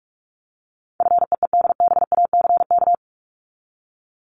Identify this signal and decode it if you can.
Morse code, heard as a keyed tone, about 44 wpm, 700 Hz. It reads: FEEDBACK
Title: Top 1501-1750 Words 45wpm